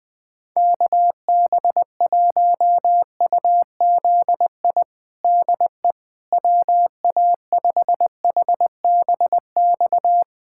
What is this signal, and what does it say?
Morse code, heard as a keyed tone, about 20 wpm, 700 Hz. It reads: KB1UZI DE WA5HBX
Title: USA Callsign DE Callsign 20wpm